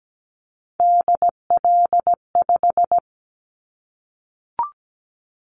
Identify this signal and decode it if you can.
Morse code, heard as a keyed tone, about 17 wpm, 700 Hz. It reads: DL5
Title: Callsign Prefixes for Top 50 DXCC - 17wpm